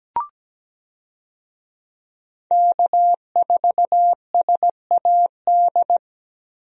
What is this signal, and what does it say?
Morse code, heard as a keyed tone, about 17 wpm, 700 Hz. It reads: K4SAD
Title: USA Call Signs - 5 Characters - Repeated 3x - 17wpm